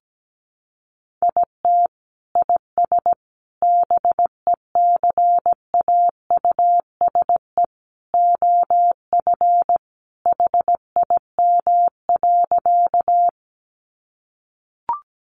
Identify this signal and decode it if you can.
Morse code, heard as a keyed tone, about 17 wpm, 700 Hz. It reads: IT IS BECAUSE OF HIM.
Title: Sentences from Top 600 Words 17wpm